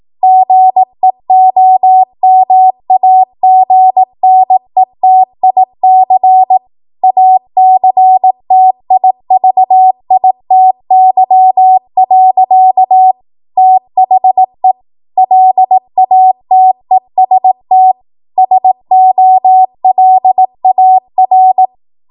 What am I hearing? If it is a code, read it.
Morse code, heard as a keyed tone, about 18 wpm, 750 Hz. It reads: GEOMAGNETIC ACTIVITY. THE LATEST SOLAR
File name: Morse / 18WPM